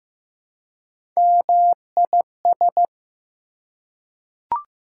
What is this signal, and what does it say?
Morse code, heard as a keyed tone, about 15 wpm, 700 Hz. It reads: MIS
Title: Prefixes and Suffixes 15wpm